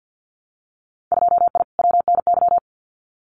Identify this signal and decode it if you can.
Morse code, heard as a keyed tone, about 50 wpm, 700 Hz. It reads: 2I WNY
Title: Winter Field Day Exchanges Encore 50wpm